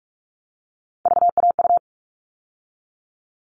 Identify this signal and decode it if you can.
Morse code, heard as a keyed tone, about 44 wpm, 700 Hz. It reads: VAU
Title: All Cut Numbers - 3-Digit 45wpm